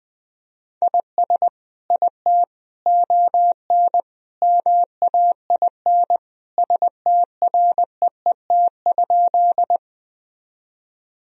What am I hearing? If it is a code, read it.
Morse code, heard as a keyed tone, about 20 wpm, 700 Hz. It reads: IS IT ON MAIN STREET?
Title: Sentences from Top 1000 Words - Review Part 2 - 20wpm